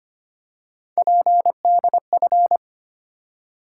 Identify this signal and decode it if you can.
Morse code, heard as a keyed tone, about 25 wpm, 700 Hz. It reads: PDF
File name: practice / 25wpm / Easy Words with Hints - 25wpm